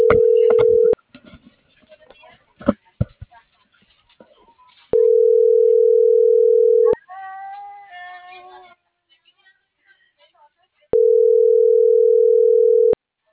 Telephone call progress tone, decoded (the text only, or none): ringback tone